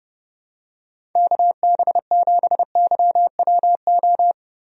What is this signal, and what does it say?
Morse code, heard as a keyed tone, about 30 wpm, 700 Hz. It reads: KB7YWO
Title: USA Call Signs - 6 Characters - 30wpm